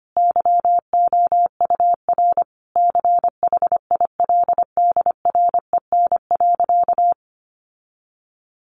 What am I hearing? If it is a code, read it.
Morse code, heard as a keyed tone, about 25 wpm, 700 Hz. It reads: YOUR CHILDREN.
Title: Sentences from Top 1200 Words 25wpm